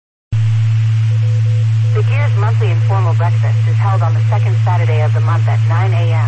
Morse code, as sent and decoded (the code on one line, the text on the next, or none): none